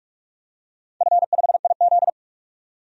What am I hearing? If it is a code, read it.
Morse code, heard as a keyed tone, about 46 wpm, 700 Hz. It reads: F5IZ
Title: DXCC-100 Call Signs - 4 Characters - 45wpm